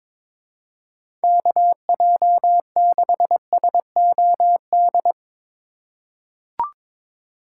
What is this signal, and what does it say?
Morse code, heard as a keyed tone, about 22 wpm, 700 Hz. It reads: KJ6SOD
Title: USA Call Signs - 6 Characters - 3x Character Spacing - 22wpm